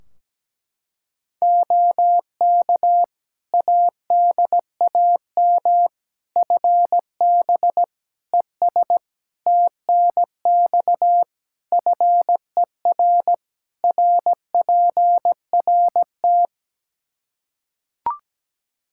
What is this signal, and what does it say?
Morse code, heard as a keyed tone, about 17 wpm, 700 Hz. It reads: OK ADAM FB ES TNX FER RPRT
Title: Practice QSO - Exchange 2 - Info - 17wpm